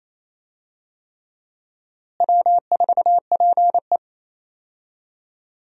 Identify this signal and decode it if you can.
Morse code, heard as a keyed tone, about 28 wpm, 700 Hz. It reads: W4PE
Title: USA Call Signs - 4 Characters - Repeated 3x - 28wpm